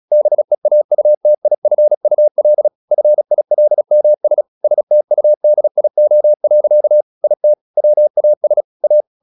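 Morse code, heard as a keyed tone, about 36 wpm, 600 Hz. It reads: BEAUTIFUL FILMS STUDIO. IT WAS A
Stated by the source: Morse practice at 36 wpm, text of Ken Follett's novel Never, from CW med Gnister.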